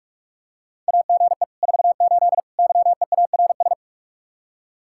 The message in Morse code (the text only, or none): AGE 48 YEARS